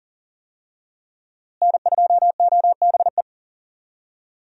Morse code, heard as a keyed tone, about 40 wpm, 700 Hz. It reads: N2OBE
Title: USA Call Signs - 5 Characters - Repeated 3x - 40wpm